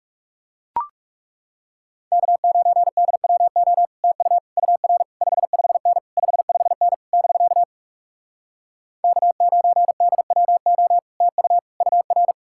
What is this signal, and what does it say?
Morse code, heard as a keyed tone, about 40 wpm, 700 Hz. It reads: K9DWO TU UR 55N 55N <BK> K9DWO TU UR